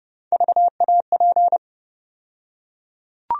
Morse code, heard as a keyed tone, about 32 wpm, 700 Hz. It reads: VAP
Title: USA Call Sign Postfixes - 3 Letters 30wpm